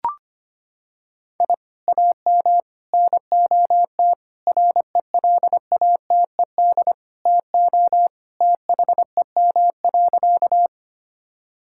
Morse code, heard as a keyed tone, about 25 wpm, 700 Hz. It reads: I AM NOT RELATED TO THEM.
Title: Sentences from Top 1000 Words - Review Part 5 - 25wpm